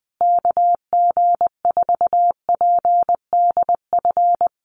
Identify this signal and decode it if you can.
Morse code, heard as a keyed tone, about 20 wpm, 700 Hz. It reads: KG4PDF
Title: USA Call Signs Repeated 3x 20wpm